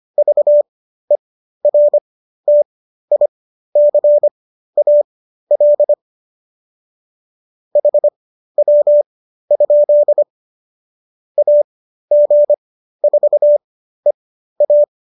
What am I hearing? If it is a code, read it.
Morse code, heard as a keyed tone, about 25 wpm, spaced as for 13 wpm, 590 Hz. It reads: VERTICAL HW? AG4EA